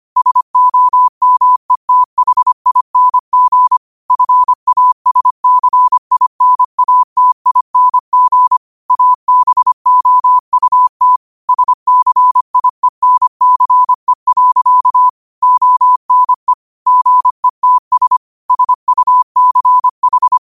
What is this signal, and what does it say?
Morse code, heard as a keyed tone, about 25 wpm, 1000 Hz. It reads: SOMETHING FASCINATING ABOUT SCIENCE. ONE GETS SUCH
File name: qotdfiles / 25wpm